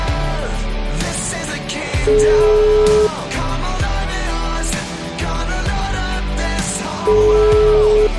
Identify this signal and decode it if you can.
Ringback tone (Europe, 425 Hz), with noise.